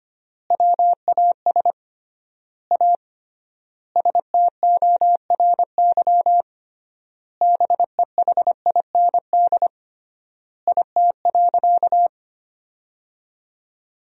Morse code, heard as a keyed tone, about 25 wpm, 700 Hz. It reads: WAS A STORY BEHIND IT.
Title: 3x Word-Spacing - Sentences from Top 500 Words - 25wpm